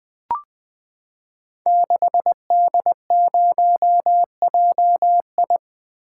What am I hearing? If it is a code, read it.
Morse code, heard as a keyed tone, about 21 wpm, 700 Hz. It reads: 6D0JI
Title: Code Groups - 5 Letter:Numbers - 20wpm